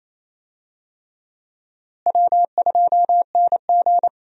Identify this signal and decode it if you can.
Morse code, heard as a keyed tone, about 28 wpm, 700 Hz. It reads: W2NG
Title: USA Call Signs - 4 Characters - Repeated 3x - 28wpm